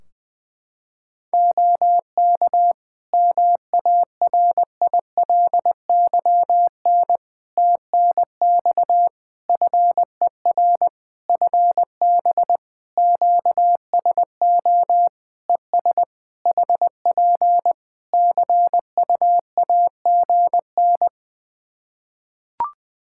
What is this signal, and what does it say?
Morse code, heard as a keyed tone, about 20 wpm, 700 Hz. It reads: OK MARILYN TNX FER FB QSO ES HP CUAGN
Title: Practice QSO - Exchange 4 - Ending - 20wpm